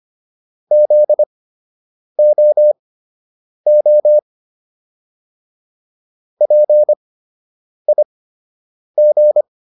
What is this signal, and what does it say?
Morse code, heard as a keyed tone, about 25 wpm, spaced as for 8 wpm, 600 Hz. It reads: ZOO PIG